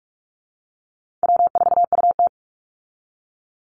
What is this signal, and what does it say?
Morse code, heard as a keyed tone, about 44 wpm, 700 Hz. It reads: W4UT
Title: Call Signs Repeated 3x - 45wpm